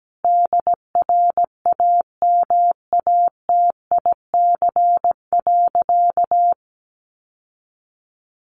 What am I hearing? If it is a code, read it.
Morse code, heard as a keyed tone, about 17 wpm, 700 Hz. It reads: DRAMATIC.
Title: Sentences from Top 1300 Words 17wpm